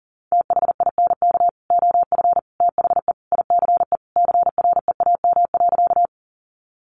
Morse code, heard as a keyed tone, about 40 wpm, 700 Hz. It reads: THINK OF THE ICE CREAM.
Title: Sentences from Top 1500 Words 40wpm